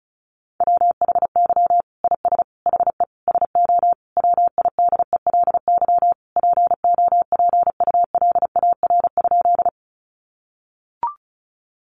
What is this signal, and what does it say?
Morse code, heard as a keyed tone, about 34 wpm, 700 Hz. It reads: WHY IS HE SO WIDELY POPULAR?
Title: Sentences from Top 1200 Words 35wpm